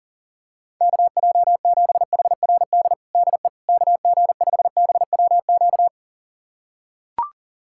Morse code, heard as a keyed tone, about 40 wpm, 700 Hz. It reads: KJ7HRD DE KG5BWQ
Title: USA Callsign DE Callsign 40wpm